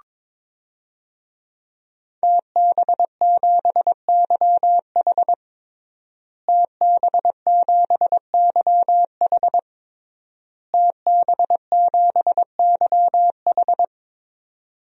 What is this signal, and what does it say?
Morse code, heard as a keyed tone, about 22 wpm, 700 Hz. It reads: TB7YH TB7YH TB7YH
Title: DXCC-100 Call Signs - 5 Characters - Repeated 3x - 22wpm